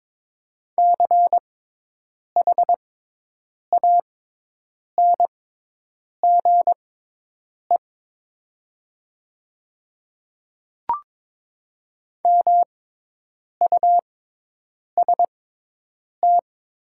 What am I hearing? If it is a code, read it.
Morse code, heard as a keyed tone, about 22 wpm, spaced as for 8 wpm, 700 Hz. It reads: CHANGE MUST
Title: 6x Character Spacing - Top 200 Words - 22wpm